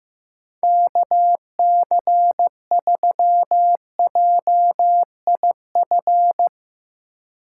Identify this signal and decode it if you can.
Morse code, heard as a keyed tone, about 15 wpm, 700 Hz. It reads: KC3JIF
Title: USA Call Signs - 6 Characters - 3x Character Spacing - 15wpm